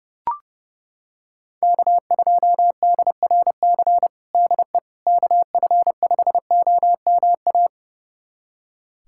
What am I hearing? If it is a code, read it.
Morse code, heard as a keyed tone, about 32 wpm, 700 Hz. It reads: K2DRC DE KF5OMA
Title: USA Callsign DE Callsign 30wpm